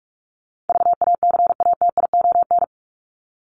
Morse code, heard as a keyed tone, about 44 wpm, 700 Hz. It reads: VACATION
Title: Top 1501-1750 Words 45wpm